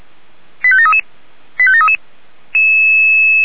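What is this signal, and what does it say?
Five-tone selective call (ZVEI), heard as quick runs of five tones, tones of about 70 ms. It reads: 85733 85733
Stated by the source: ZVEI five-tone call received by SDR.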